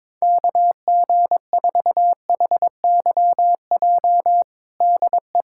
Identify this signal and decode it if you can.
Morse code, heard as a keyed tone, about 22 wpm, 700 Hz. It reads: KG4HYJ DE
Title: USA Callsign DE Callsign - 6 Characters - 22wpm